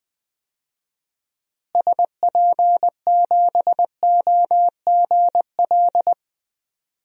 Morse code, heard as a keyed tone, about 21 wpm, 700 Hz. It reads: SP7OGL
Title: DXCC-100 Call Signs - 6 Characters - Repeated 2x - 20wpm